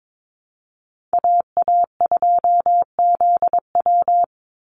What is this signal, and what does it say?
Morse code, heard as a keyed tone, about 22 wpm, 700 Hz. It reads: AA2ZW
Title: Call Signs Repeated 2x - 22wpm